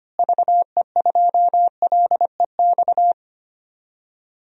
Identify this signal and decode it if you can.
Morse code, heard as a keyed tone, about 25 wpm, 700 Hz. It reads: VE2LEX
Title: DXCC-100 Call Signs - 6 Characters - Repeated 2x - 25wpm